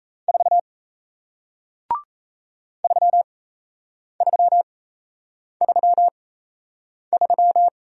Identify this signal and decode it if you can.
Morse code, heard as a keyed tone, about 38 wpm, spaced as for 8 wpm, 700 Hz. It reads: 4 3333